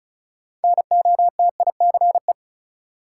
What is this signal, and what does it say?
Morse code, heard as a keyed tone, about 36 wpm, 700 Hz. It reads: NOTICE
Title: Sentences from Top 400 Words - New Word Drill - 35wpm